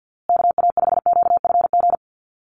Morse code, heard as a keyed tone, about 50 wpm, 700 Hz. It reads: KA5QFG
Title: USA Call Signs Repeated 3x 50wpm